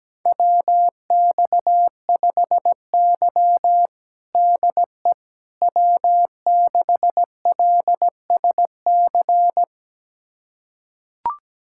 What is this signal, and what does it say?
Morse code, heard as a keyed tone, about 17 wpm, 700 Hz. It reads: WX5Y DE W6LSC
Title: USA Callsign DE Callsign 17wpm